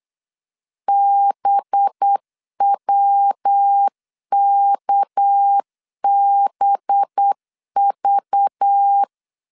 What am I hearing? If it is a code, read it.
Morse code, heard as a keyed tone, about 8 wpm, 800 Hz. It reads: BWKBV
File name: LEZIONI / 6 PKX0 / 6 PK 8wpm 3003